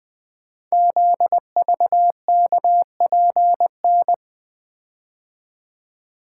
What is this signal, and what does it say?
Morse code, heard as a keyed tone, about 20 wpm, 700 Hz. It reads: ZVKPN
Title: Code Groups - 5 Letters - 20wpm